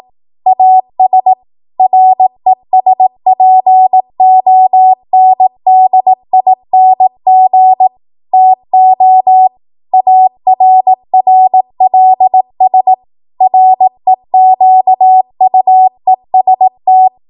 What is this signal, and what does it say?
Morse code, heard as a keyed tone, about 18 wpm, 750 Hz. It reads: AS RESPONDING TO ARRLS REQUEST
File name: Morse / 18WPM